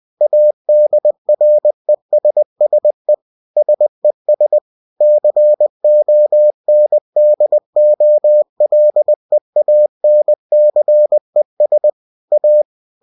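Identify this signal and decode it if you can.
Morse code, heard as a keyed tone, about 20 wpm, 600 Hz. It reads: ADRESSE SES CONDOLEANCES A
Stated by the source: Bulletin sent in CW.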